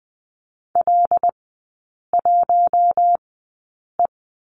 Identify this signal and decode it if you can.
Morse code, heard as a keyed tone, about 19 wpm, spaced as for 8 wpm, 700 Hz. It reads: L1E
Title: 127 Lesson 21e - Number 6 - Three Characters 20wpm